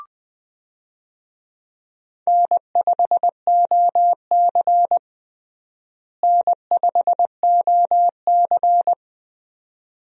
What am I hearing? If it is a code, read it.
Morse code, heard as a keyed tone, about 20 wpm, 700 Hz. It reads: N5OC N5OC
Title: USA Call Signs - 4 Characters - Repeated 2x - 20wpm